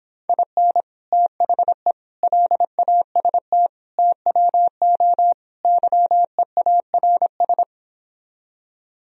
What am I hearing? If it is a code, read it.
Morse code, heard as a keyed tone, about 26 wpm, 700 Hz. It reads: IN THE LAST TWO YEARS